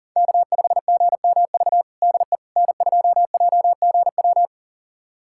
Morse code, heard as a keyed tone, about 40 wpm, 700 Hz. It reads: K5GMV DE N2JGW
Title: USA Callsign DE Callsign - 5 Characters - 40wpm